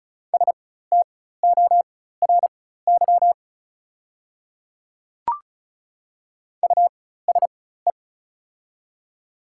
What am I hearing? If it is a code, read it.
Morse code, heard as a keyed tone, about 36 wpm, spaced as for 16 wpm, 690 Hz. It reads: STORY USE
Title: 4x Character Spacing - Top 300 Words - 35wpm